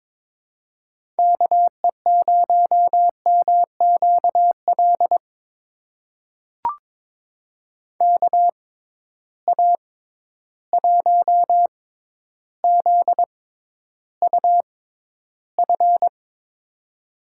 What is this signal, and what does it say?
Morse code, heard as a keyed tone, about 22 wpm, spaced as for 11 wpm, 700 Hz. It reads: KE0MQL KA1ZUF
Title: USA Call Signs - 6 Characters - 6x Character Spacing - 22wpm